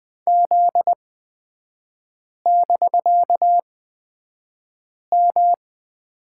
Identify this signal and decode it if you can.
Morse code, heard as a keyed tone, about 20 wpm, spaced as for 5 wpm, 700 Hz. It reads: Z<BK>M
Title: 255 Lesson 41e - Period - Three Characters 20wpm